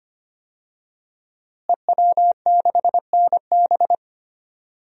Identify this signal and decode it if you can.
Morse code, heard as a keyed tone, about 25 wpm, 700 Hz. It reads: EW6NB
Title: DXCC-100 Call Signs - 5 Characters - Repeated 3x - 25wpm